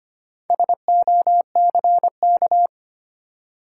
Morse code, heard as a keyed tone, about 25 wpm, 700 Hz. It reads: SOCK
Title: Four Letter Words - 25wpm